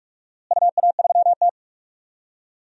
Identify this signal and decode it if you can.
Morse code, heard as a keyed tone, about 46 wpm, 700 Hz. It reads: UA3T
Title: DXCC-100 Call Signs - 4 Characters - Repeated 2x - 45wpm